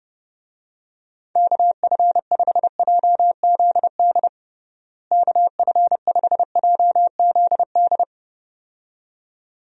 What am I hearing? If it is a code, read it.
Morse code, heard as a keyed tone, about 30 wpm, 700 Hz. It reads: KF5JZD KF5JZD
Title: USA Call Signs - 6 Characters - Repeated 2x - 30wpm